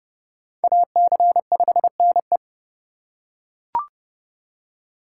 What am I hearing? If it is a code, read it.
Morse code, heard as a keyed tone, about 29 wpm, 700 Hz. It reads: AC5NE